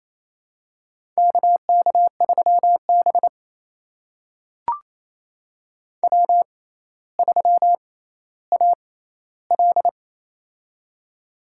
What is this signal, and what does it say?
Morse code, heard as a keyed tone, about 27 wpm, spaced as for 14 wpm, 700 Hz. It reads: KK3B W3AL
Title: USA Call Signs - 4 Characters - 6x Character Spacing - 28wpm